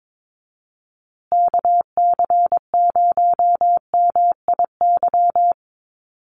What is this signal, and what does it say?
Morse code, heard as a keyed tone, about 22 wpm, 700 Hz. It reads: KC0MIY
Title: USA Call Signs Repeated 3x 22wpm